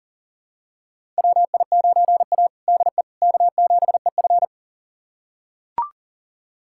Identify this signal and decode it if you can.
Morse code, heard as a keyed tone, about 40 wpm, 700 Hz. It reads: WI9A DE K7EF
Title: USA Callsign DE Callsign 40wpm